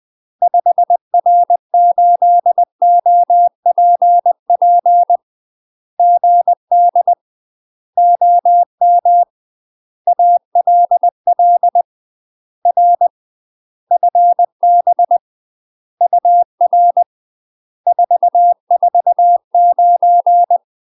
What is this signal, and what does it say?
Morse code, heard as a keyed tone, about 20 wpm, 700 Hz. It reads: 5R8OPP GD OM ALL R FB UR 449
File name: UK MASTER QSOs TRAINING / 20WPM / UK MASTER QSOs 0049